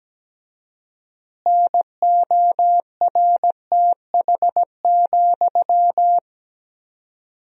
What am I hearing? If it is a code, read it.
Morse code, heard as a keyed tone, about 17 wpm, 700 Hz. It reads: NORTH,
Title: 4x Word-Spacing - Sentences from Top 400 Words - 17wpm